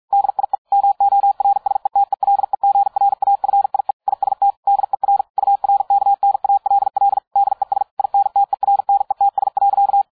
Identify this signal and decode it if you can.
Morse code, heard as a keyed tone, about 53 wpm, 800 Hz. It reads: DIE MORSETELEGRAFIE IST DER URKNALL DES INTERNETS.